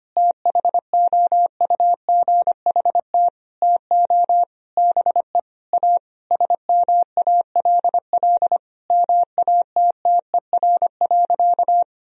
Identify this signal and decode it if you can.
Morse code, heard as a keyed tone, about 25 wpm, 700 Hz. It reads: THOUGHT TO BE A SMALL MATTER.
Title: Sentences from Top 1000 Words - Review Part 1 - 25wpm